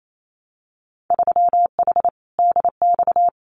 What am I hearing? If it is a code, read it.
Morse code, heard as a keyed tone, about 28 wpm, 700 Hz. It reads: 3H DX
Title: Winter Field Day Exchanges Encore 28wpm